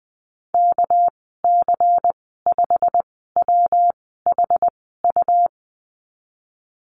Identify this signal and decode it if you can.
Morse code, heard as a keyed tone, about 20 wpm, spaced as for 14 wpm, 700 Hz. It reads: KC5WHU
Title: 213 Lesson 34g - Number 8 - Call Signs 20wpm